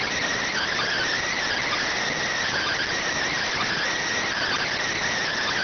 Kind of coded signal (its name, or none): Emergency Alert System (SAME) tones